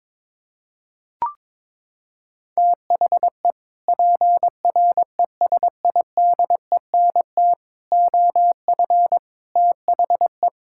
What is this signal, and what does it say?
Morse code, heard as a keyed tone, about 22 wpm, 700 Hz. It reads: THE PRESIDENT OF THE
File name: practice / 22wpm / Sets of 6 Words - 22wpm